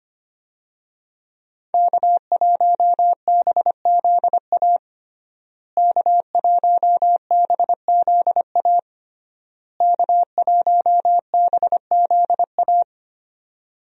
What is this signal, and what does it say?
Morse code, heard as a keyed tone, about 25 wpm, 700 Hz. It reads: K1BZA K1BZA K1BZA
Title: USA Call Signs - 5 Characters - Repeated 3x - 25wpm